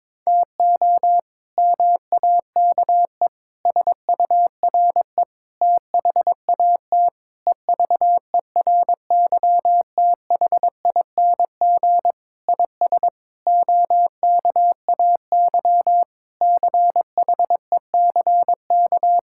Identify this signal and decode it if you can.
Morse code, heard as a keyed tone, about 22 wpm, 700 Hz. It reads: TO MAKE SURE THAT EVERYTHING IS OKAY CHECK
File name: practice / 22wpm / Sentences with 4-Word Phrases - 22wpm